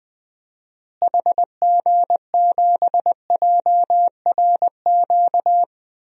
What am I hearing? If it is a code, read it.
Morse code, heard as a keyed tone, about 21 wpm, 700 Hz. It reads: HG7JRQ
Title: DXCC-100 Call Signs - 6 Characters - Repeated 2x - 20wpm